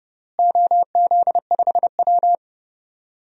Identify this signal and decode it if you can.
Morse code, heard as a keyed tone, about 32 wpm, 700 Hz. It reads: OZ5W
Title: DXCC-100 Call Signs - 4 Characters - Repeated 3x - 30wpm